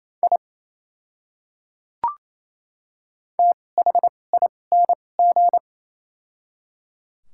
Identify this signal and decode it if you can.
Morse code, heard as a keyed tone, about 27 wpm, spaced as for 20 wpm, 700 Hz. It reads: I THING
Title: 2x Character Spacing - Top 100 Words - 28wpm